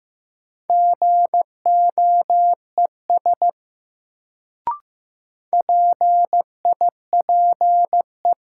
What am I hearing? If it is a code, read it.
Morse code, heard as a keyed tone, about 15 wpm, 700 Hz. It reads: GOES PIPE
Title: Sentences from Top 1500 Words - New Word Drill Alt - 15wpm